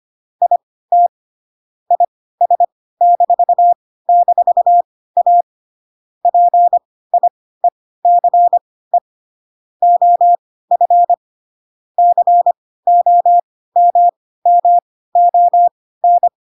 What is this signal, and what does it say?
Morse code, heard as a keyed tone, about 25 wpm, spaced as for 16 wpm, 700 Hz. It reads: IT IS--A PIECE OF COMMON
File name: Pinocchio / Pinocchio ch1 10 25 ik1hge